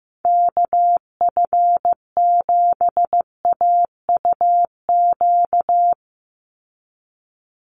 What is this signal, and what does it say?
Morse code, heard as a keyed tone, about 15 wpm, 700 Hz. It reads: KF7AUQ
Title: USA Call Signs 15wpm